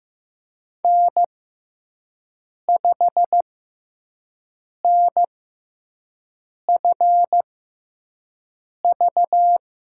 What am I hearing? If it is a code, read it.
Morse code, heard as a keyed tone, about 15 wpm, spaced as for 5 wpm, 700 Hz. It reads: N5NFV
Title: USA Call Signs - 5 Characters - 6x Character Spacing - 15wpm